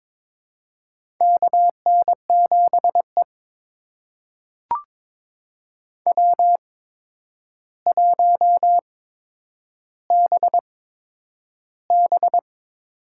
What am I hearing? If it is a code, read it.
Morse code, heard as a keyed tone, about 22 wpm, spaced as for 9 wpm, 700 Hz. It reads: KN7E W1BB
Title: USA Call Signs - 4 Characters - 8x Character Spacing - 22wpm